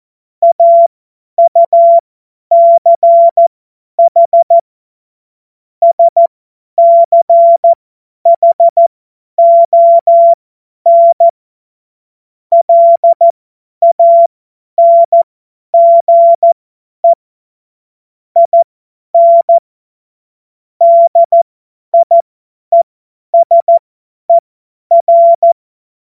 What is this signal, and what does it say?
Morse code, heard as a keyed tone, about 14 wpm, spaced as for 10 wpm, 680 Hz. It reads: AUCH SCHON LANGE IN DIESER